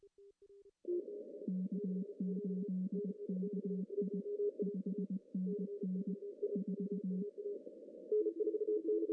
Morse code, heard as a keyed tone, about 20 wpm, 190 Hz. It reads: K8KI 5NN 4
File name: CW / cqww-cw-k5tr